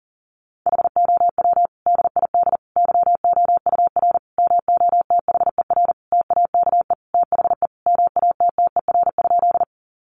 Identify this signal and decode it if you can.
Morse code, heard as a keyed tone, about 40 wpm, 700 Hz. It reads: HOW DID YOUR MOTHER TAKE THE MATTER?
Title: Sentences from Top 700 Words 40wpm